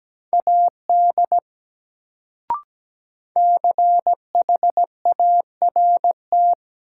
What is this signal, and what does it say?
Morse code, heard as a keyed tone, about 17 wpm, 700 Hz. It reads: AD CHART
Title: Sentences from Top 1300 Words - New Word Drill - 17wpm